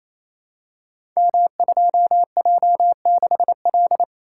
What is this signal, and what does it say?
Morse code, heard as a keyed tone, about 27 wpm, 700 Hz. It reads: M2J6L
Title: Code Groups - 5 Letter:Numbers - 28wpm